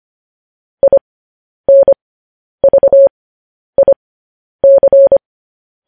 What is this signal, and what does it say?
Morse code, heard as a keyed tone, about 25 wpm, spaced as for 10 wpm, 550 Hz. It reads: INVIC